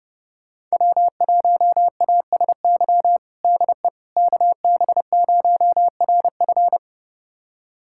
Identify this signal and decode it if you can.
Morse code, heard as a keyed tone, about 32 wpm, 700 Hz. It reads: W1ASY DE KB0RF
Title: USA Callsign DE Callsign - 5 Characters - 30wpm